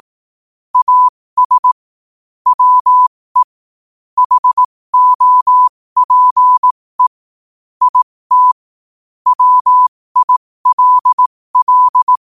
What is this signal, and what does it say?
Morse code, heard as a keyed tone, about 18 wpm, spaced as for 15 wpm, 1000 Hz. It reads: AS WE HOPE IT WILL
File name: qotdfiles / 15wpm